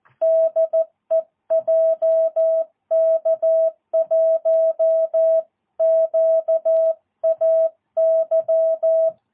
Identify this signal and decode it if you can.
Morse code, heard as a keyed tone, about 14 wpm, 650 Hz. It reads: DE JK1QAY
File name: jh8cbh / 2022cw